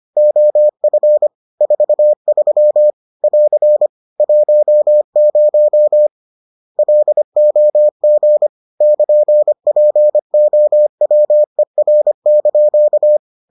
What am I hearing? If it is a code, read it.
Morse code, heard as a keyed tone, about 25 wpm, 600 Hz. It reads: OF 43 <AR> 10 LOG <KN>POWER)